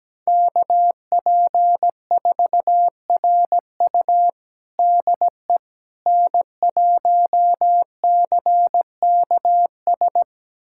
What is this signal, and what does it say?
Morse code, heard as a keyed tone, about 17 wpm, 700 Hz. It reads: KP4RU DE N1CKS
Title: USA Callsign DE Callsign - 5 Characters - 17wpm